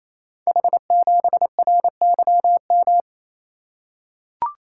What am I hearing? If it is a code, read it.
Morse code, heard as a keyed tone, about 27 wpm, 700 Hz. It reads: H7RYM
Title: Code Groups - 5 Letter:Numbers - 28wpm